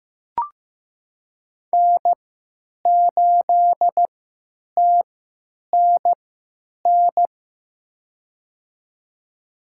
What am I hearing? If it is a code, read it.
Morse code, heard as a keyed tone, about 15 wpm, spaced as for 8 wpm, 700 Hz. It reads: N8TNN
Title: USA Call Signs - 5 Characters - 3x Character Spacing - 15wpm